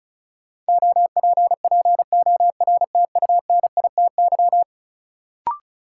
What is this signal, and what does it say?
Morse code, heard as a keyed tone, about 34 wpm, 700 Hz. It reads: OPPORTUNITY
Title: Sentences from Top 1000 Words - New Word Drill - 35wpm